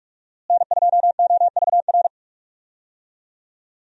Morse code, heard as a keyed tone, about 44 wpm, 700 Hz. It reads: N2OVR
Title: USA Call Signs - 5 Characters - Repeated 3x - 45wpm